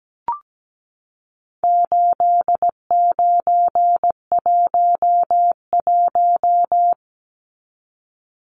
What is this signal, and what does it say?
Morse code, heard as a keyed tone, about 17 wpm, 700 Hz. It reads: 8911
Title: Numbers - 4-Digit 17wpm